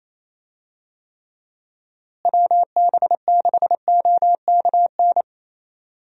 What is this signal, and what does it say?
Morse code, heard as a keyed tone, about 28 wpm, 700 Hz. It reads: WB6OKN
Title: USA Call Signs - 6 Characters - Repeated 3x - 28wpm